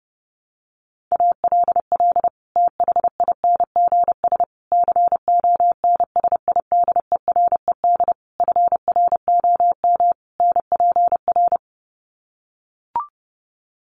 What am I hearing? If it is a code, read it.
Morse code, heard as a keyed tone, about 30 wpm, 700 Hz. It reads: ALL THINGS CONSIDERED FROM NPR
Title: Sets of 5 Words Encore 30wpm